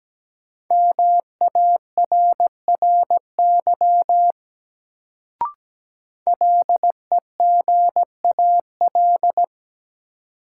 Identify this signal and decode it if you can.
Morse code, heard as a keyed tone, about 17 wpm, 700 Hz. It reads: MARRY LEGAL
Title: Sentences from Top 1200 Words - New Word Drill Alt - 17wpm